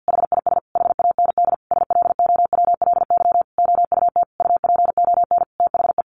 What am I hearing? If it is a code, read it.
Morse code, heard as a keyed tone, about 50 wpm, 700 Hz. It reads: HIS HAND SLOWLY OUT UPON THE
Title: A Sound Of Thunder 50wpm